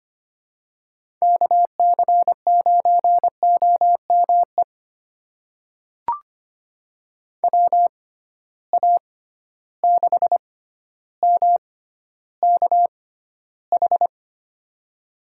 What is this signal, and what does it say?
Morse code, heard as a keyed tone, about 25 wpm, spaced as for 13 wpm, 700 Hz. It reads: KC9OME WA6MKH
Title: USA Call Signs - 6 Characters - 6x Character Spacing - 25wpm